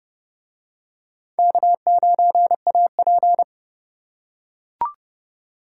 Morse code, heard as a keyed tone, about 28 wpm, 700 Hz. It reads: K9AP